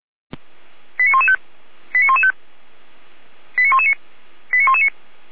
Five-tone selective call (ZVEI), heard as quick runs of five tones, tones of about 70 ms. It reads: 89115 89115 89119 89119